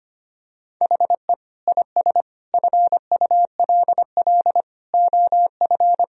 Morse code, heard as a keyed tone, about 25 wpm, 700 Hz. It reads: HE IS FULL OF
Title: Sentences from Top 1000 Words - Review Part 3 - 25wpm